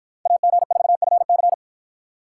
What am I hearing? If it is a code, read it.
Morse code, heard as a keyed tone, about 48 wpm, 700 Hz. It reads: AG4FC